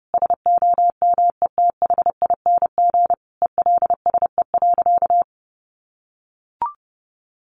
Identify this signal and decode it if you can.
Morse code, heard as a keyed tone, about 30 wpm, 700 Hz. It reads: SOMETHING ELSE.
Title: Sentences from Top 800 Words 30wpm